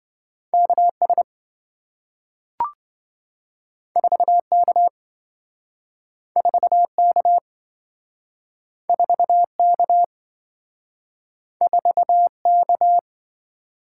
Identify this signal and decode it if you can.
Morse code, heard as a keyed tone, about 27 wpm, 700 Hz. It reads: KS 4K 4K 4K 4K